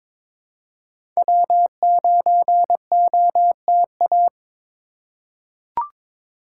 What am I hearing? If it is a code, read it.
Morse code, heard as a keyed tone, about 22 wpm, 700 Hz. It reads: W9OTA
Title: USA Call Signs - 5 Characters - 2x Character Spacing - 22wpm